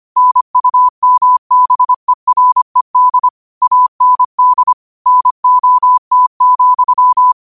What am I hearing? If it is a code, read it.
Morse code, heard as a keyed tone, about 25 wpm, 1000 Hz. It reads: NUMBERED AND NOT,
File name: 25wpm